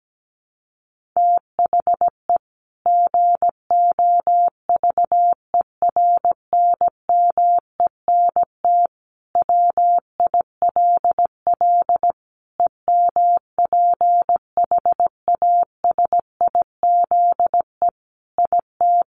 Morse code, heard as a keyed tone, about 17 wpm, 700 Hz. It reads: THE GOVERNMENT WILL EMPHASIZE IT
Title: Sentences from Top 1500 Words 17wpm